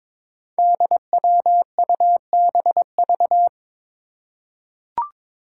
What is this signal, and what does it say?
Morse code, heard as a keyed tone, about 22 wpm, 700 Hz. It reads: DWUBV
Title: Five Challenging Letter-Number Combinations 22wpm